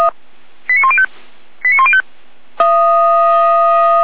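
Five-tone selective call (ZVEI), heard as quick runs of five tones, tones of about 70 ms. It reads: 89105 89105
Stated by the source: ZVEI five-tone call received by SDR.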